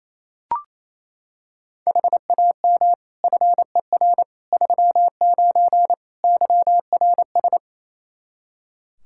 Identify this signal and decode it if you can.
Morse code, heard as a keyed tone, about 28 wpm, 700 Hz. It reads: HAM FER 39 YRS
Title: Practice QSO - Exchange 3 - More Info - 28wpm